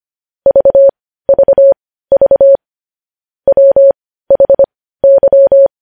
Morse code, heard as a keyed tone, about 25 wpm, spaced as for 15 wpm, 550 Hz. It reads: VVV WHY
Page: news headlines in morse code at 15 wpm